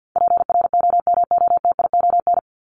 Morse code, heard as a keyed tone, about 50 wpm, 700 Hz. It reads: PROMOTION
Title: Top 1501-1750 Words 50wpm